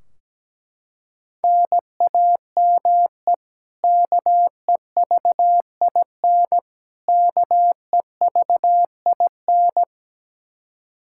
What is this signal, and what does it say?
Morse code, heard as a keyed tone, about 17 wpm, 700 Hz. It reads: NAME KEVIN KEVIN
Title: Practice QSO - Exchange 1 - Beginning - 17wpm